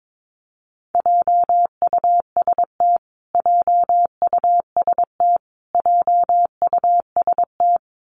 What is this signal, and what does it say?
Morse code, heard as a keyed tone, about 22 wpm, 700 Hz. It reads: JUST JUST JUST
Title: Top 500 Words Repeated 22wpm